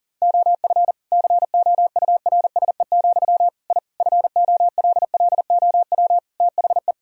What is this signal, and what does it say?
Morse code, heard as a keyed tone, about 40 wpm, 700 Hz. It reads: OF COURSE, I FOLLOW THE
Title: Sentences from Top 1000 Words - Review Part 4 - 40wpm